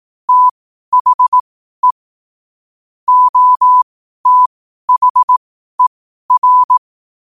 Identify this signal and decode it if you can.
Morse code, heard as a keyed tone, about 18 wpm, spaced as for 12 wpm, 1000 Hz. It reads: THE OTHER
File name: qotdfiles / 12wpm